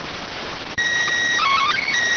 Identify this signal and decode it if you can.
SSTV (slow-scan television), PD 290.